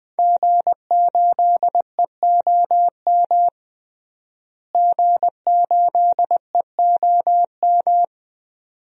Morse code, heard as a keyed tone, about 20 wpm, 700 Hz. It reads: G8EOM G8EOM
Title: DXCC-100 Call Signs - 5 Characters - Repeated 2x - 20wpm